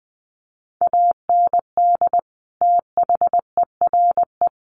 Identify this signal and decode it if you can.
Morse code, heard as a keyed tone, about 20 wpm, 700 Hz. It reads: AND THERE
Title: Sentences from Top 1400 Words 20wpm